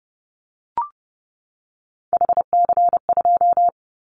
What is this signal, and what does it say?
Morse code, heard as a keyed tone, about 30 wpm, 700 Hz. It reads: HC2
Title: Callsign Prefixes for Top 100 DXCC - 30wpm